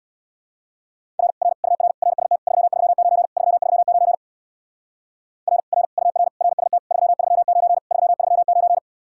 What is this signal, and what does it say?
Morse code, heard as a keyed tone, about 59 wpm, 700 Hz. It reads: R R UR RST 339 339 R R UR RST 339 339
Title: Speed-Racing Common QSO Phrases - 50wpm